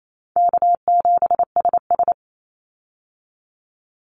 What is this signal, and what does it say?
Morse code, heard as a keyed tone, about 28 wpm, 700 Hz. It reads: K7SS
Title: Easy Call Signs Repeated 2x - 28wpm